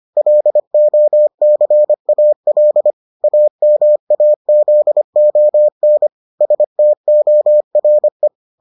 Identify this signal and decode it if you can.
Morse code, heard as a keyed tone, about 25 wpm, 600 Hz. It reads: LOCAL AMAZON STORE